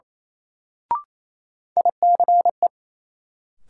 Morse code, heard as a keyed tone, about 28 wpm, 700 Hz. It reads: ICE
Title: Sentences from Top 1000 Words - New Word Drill - 28wpm